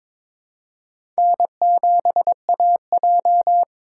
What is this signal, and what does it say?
Morse code, heard as a keyed tone, about 22 wpm, 700 Hz. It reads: N7AJ
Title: USA Call Signs - 4 Characters - 2x Character Spacing - 22wpm